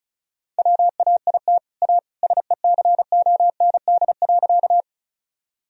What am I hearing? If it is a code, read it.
Morse code, heard as a keyed tone, about 36 wpm, 700 Hz. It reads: WAIT A SECOND.
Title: Sentences from Top 1000 Words - Review Part 1 - 35wpm